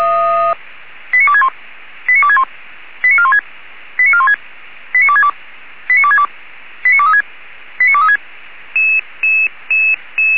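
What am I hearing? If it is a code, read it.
Five-tone selective call (ZVEI), heard as quick runs of five tones, tones of about 70 ms. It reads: 89261 89261 89416 89416 89262 89262 89236 89236
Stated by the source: ZVEI five-tone selective call received on SDR.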